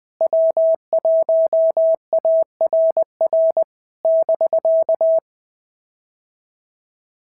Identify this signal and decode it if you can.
Morse code, heard as a keyed tone, about 20 wpm, 650 Hz. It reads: W1ARR <BK>
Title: POTA QSO Practice - 20wpm